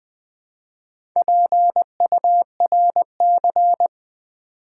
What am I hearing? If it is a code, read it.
Morse code, heard as a keyed tone, about 20 wpm, 700 Hz. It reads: PURC
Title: Build A Word - 2 Letters At A Time - 8-Letter Words - 20wpm